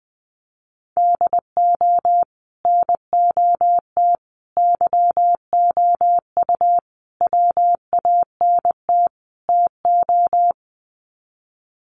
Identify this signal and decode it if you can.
Morse code, heard as a keyed tone, about 20 wpm, 700 Hz. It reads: DO NOT YOU WANT TO
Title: Sets of 5 Words Encore2 20wpm